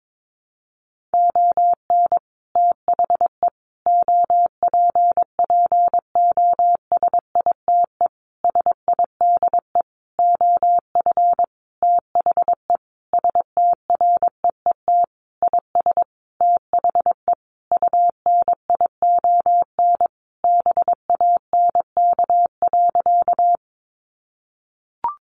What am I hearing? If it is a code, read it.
Morse code, heard as a keyed tone, about 22 wpm, 700 Hz. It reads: ON THE OPPOSITE SIDE OF THE STREET IS THE UNION BANK.
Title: Sentences from Top 1400 Words 22wpm